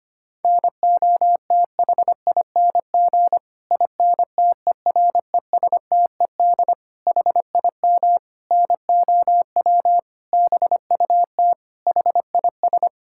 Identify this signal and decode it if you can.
Morse code, heard as a keyed tone, about 25 wpm, 700 Hz. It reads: NOTHING INTERESTED HIM NOW BUT HIS
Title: Sentences from Top 1000 Words - Review Part 6 - 25wpm